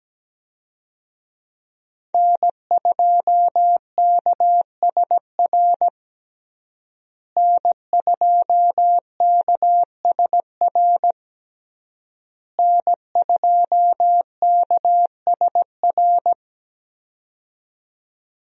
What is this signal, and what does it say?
Morse code, heard as a keyed tone, about 17 wpm, 700 Hz. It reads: N2KSR N2KSR N2KSR
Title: USA Call Signs - 5 Characters - Repeated 3x - 17wpm